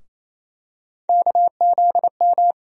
Morse code, heard as a keyed tone, about 28 wpm, 700 Hz. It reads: KZM
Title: USA Call Sign Postfixes - 3 Letters 28wpm